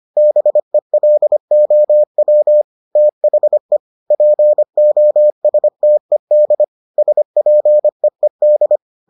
Morse code, heard as a keyed tone, about 25 wpm, 600 Hz. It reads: BELOW THE POSTED SPEED